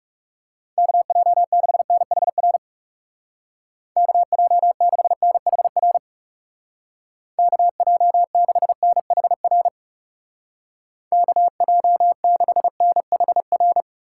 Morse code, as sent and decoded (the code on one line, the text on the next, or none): -.- .--- -.... -. .... .-. / -.- .--- -.... -. .... .-. / -.- .--- -.... -. .... .-. / -.- .--- -.... -. .... .-.
KJ6NHR KJ6NHR KJ6NHR KJ6NHR